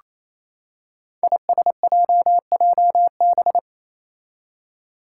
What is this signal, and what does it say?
Morse code, heard as a keyed tone, about 28 wpm, 700 Hz. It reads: ISJJB
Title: Five Challenging Letter-Number Combinations 28wpm